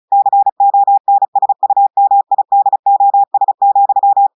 Morse code, heard as a keyed tone, about 36 wpm, 800 Hz. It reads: CONSUMIDOS,
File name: audios / portugais / portugais texte en clair 2 vitesse 35 mots minutes